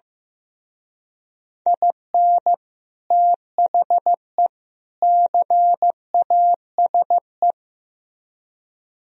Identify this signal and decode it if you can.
Morse code, heard as a keyed tone, about 16 wpm, 700 Hz. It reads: IN THE CASE
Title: Speed-Racing - Sets of 3 Words 15wpm